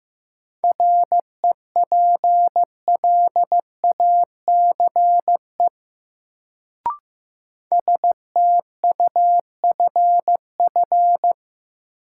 Morse code, heard as a keyed tone, about 15 wpm, 700 Hz. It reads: REPLACE STUFF
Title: Sentences from Top 1400 Words - New Word Drill Alt - 15wpm